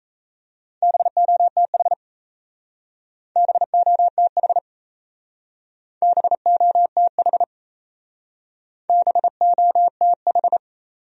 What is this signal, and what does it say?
Morse code, heard as a keyed tone, about 34 wpm, 700 Hz. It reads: BOTH BOTH BOTH BOTH
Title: Speed-Racing Top 300 Words 28wpm